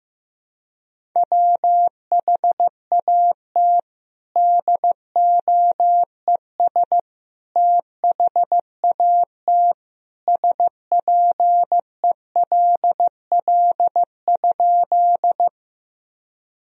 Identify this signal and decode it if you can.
Morse code, heard as a keyed tone, about 16 wpm, 700 Hz. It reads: WHAT DOES THAT SPELL?
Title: Sentences from Top 1000 Words - Review Part 1 - 15wpm